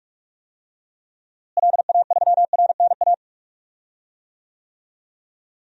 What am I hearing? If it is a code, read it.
Morse code, heard as a keyed tone, about 41 wpm, 700 Hz. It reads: LA3RNA
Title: DXCC-100 Call Signs - 6 Characters - Repeated 2x - 45wpm